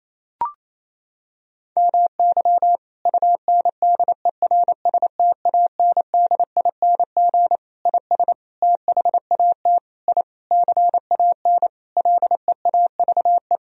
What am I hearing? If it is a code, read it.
Morse code, heard as a keyed tone, about 27 wpm, 700 Hz. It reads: MY UNDERSTANDING IS THAT I CAN LEAVE
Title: Sentences from Top 1000 Words - Review Part 3 - 28wpm